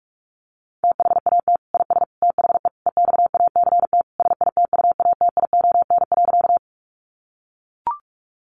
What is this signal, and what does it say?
Morse code, heard as a keyed tone, about 44 wpm, 700 Hz. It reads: THAT IS THE EXACT SITUATION.
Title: Sentences from Top 1300 Words 45wpm